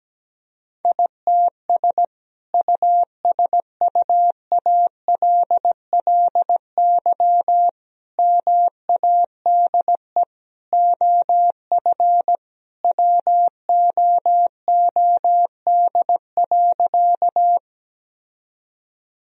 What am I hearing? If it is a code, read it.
Morse code, heard as a keyed tone, about 17 wpm, 700 Hz. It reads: ITS USUALLY MADE OF WOOD.
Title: Sentences from Top 1000 Words - Review Part 3 - 17wpm